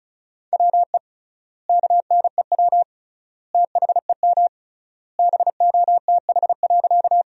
Morse code, heard as a keyed tone, about 36 wpm, 700 Hz. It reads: WE KNEW THEM BOTH.
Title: 3x Word-Spacing - Sentences from Top 400 Words - 35wpm